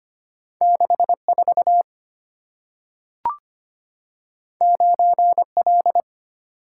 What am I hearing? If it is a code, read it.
Morse code, heard as a keyed tone, about 25 wpm, 700 Hz. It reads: 64 9L
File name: Two Letter-Number Combinations - 25wpm